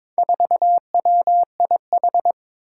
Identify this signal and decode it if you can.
Morse code, heard as a keyed tone, about 22 wpm, 700 Hz. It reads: 4WIH
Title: Four Challenging Letter-Number Combinations 22wpm